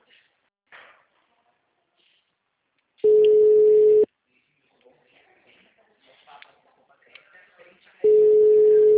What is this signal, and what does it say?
Ringback tone (Europe, 425 Hz), with speech.
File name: out- -2005- - - .37